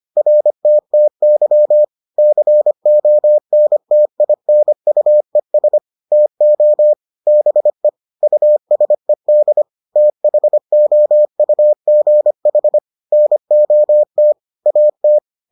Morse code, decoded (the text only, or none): RTTY CONTINUES TO BE USED THOUGH NOT AT